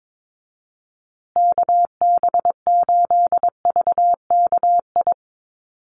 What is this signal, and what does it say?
Morse code, heard as a keyed tone, about 22 wpm, 700 Hz. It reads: KB8VKI
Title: USA Call Signs Repeated 3x 22wpm